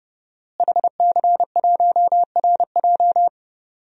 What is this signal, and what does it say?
Morse code, heard as a keyed tone, about 30 wpm, 700 Hz. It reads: HC1RJ
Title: DXCC-100 Call Signs - 5 Characters - Repeated 3x - 30wpm